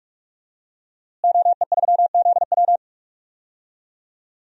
Morse code, heard as a keyed tone, about 46 wpm, 700 Hz. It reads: OE3ZW
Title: DXCC-100 Call Signs - 5 Characters - Repeated 3x - 45wpm